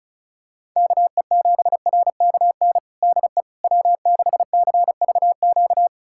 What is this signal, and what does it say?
Morse code, heard as a keyed tone, about 36 wpm, 700 Hz. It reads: KE7RKN DE W6CVQ
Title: USA Callsign DE Callsign 35wpm